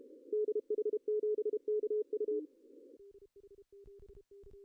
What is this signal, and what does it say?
Morse code, heard as a keyed tone, about 33 wpm, 420 Hz, with radio noise. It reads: DH7KU
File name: CW / cqww-cw-k5tr